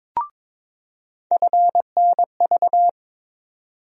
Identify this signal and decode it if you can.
Morse code, heard as a keyed tone, about 22 wpm, 700 Hz. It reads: FNV
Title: USA Call Sign Postfixes - 3 Letters 22wpm